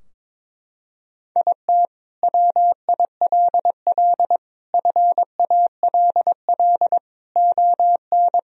Morse code, heard as a keyed tone, about 22 wpm, 700 Hz. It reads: IT WILL FALL ON
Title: Sentences from Top 1000 Words - Review Part 2 - 22wpm